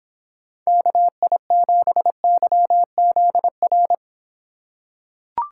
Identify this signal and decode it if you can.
Morse code, heard as a keyed tone, about 26 wpm, 700 Hz. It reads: KI7YZR